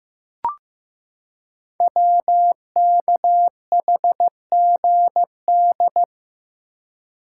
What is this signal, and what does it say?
Morse code, heard as a keyed tone, about 15 wpm, 700 Hz. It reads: WKHGD
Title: Code Groups - 5 Letters - 15wpm